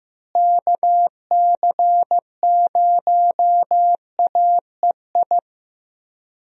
Morse code, heard as a keyed tone, about 15 wpm, 700 Hz. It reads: KC0AEI
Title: USA Call Signs - 6 Characters - Repeated 2x - 15wpm